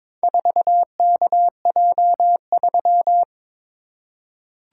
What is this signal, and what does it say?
Morse code, heard as a keyed tone, about 21 wpm, 700 Hz. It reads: 4KJ3
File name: practice / 22wpm / Four Letter-Number Combinations - 22wpm